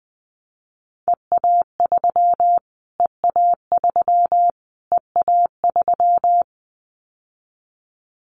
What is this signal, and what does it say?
Morse code, heard as a keyed tone, about 20 wpm, 700 Hz. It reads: EA3 EA3 EA3
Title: Callsign Prefixes for Top 25 DXCC - Repeated 3x - 20wpm